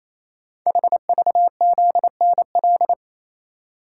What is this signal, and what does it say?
Morse code, heard as a keyed tone, about 27 wpm, 700 Hz. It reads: HVZNL
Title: Code Groups - 5 Letters - 28wpm